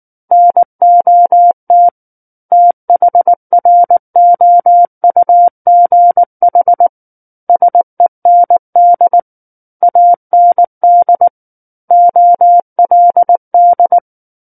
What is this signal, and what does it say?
Morse code, heard as a keyed tone, about 19 wpm, 700 Hz. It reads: NOT THROUGH SEND AND OLD
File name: Book One / 10-Chapter 10 - 19 WPM